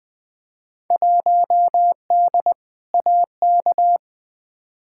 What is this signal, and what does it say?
Morse code, heard as a keyed tone, about 20 wpm, 700 Hz. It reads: 1D AK
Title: Field Day Exchanges - 20wpm